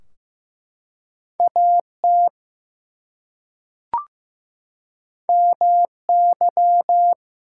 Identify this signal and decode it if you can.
Morse code, heard as a keyed tone, about 15 wpm, 700 Hz. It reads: AT MY
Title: Two Letter Words - 15wpm